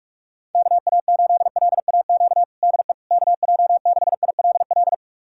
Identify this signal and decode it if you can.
Morse code, heard as a keyed tone, about 47 wpm, 700 Hz. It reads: KA8LAQ DE KJ6ILL
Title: USA Callsign DE Callsign 45wpm